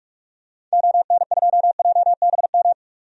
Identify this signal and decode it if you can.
Morse code, heard as a keyed tone, about 42 wpm, 700 Hz. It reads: ON2JBM
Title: DXCC-100 Call Signs - 6 Characters - 45wpm